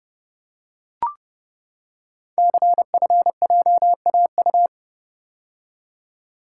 Morse code, heard as a keyed tone, about 30 wpm, 700 Hz. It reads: CFJAU
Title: Code Groups - 5 Letters - 30wpm